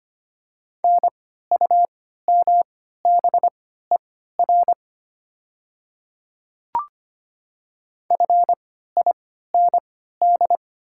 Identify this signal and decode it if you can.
Morse code, heard as a keyed tone, about 25 wpm, spaced as for 14 wpm, 700 Hz. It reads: NUMBER FIND
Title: 3x Character Spacing - Top 100 Words - 25wpm